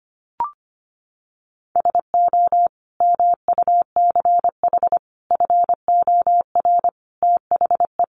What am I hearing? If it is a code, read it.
Morse code, heard as a keyed tone, about 25 wpm, 700 Hz. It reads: SO MUCH FOR THE
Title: Sentences from Top 1000 Words 25wpm